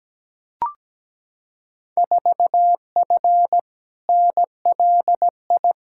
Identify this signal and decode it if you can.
Morse code, heard as a keyed tone, about 17 wpm, 700 Hz. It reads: 4F NLI
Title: Field Day Exchanges Encore - 17wpm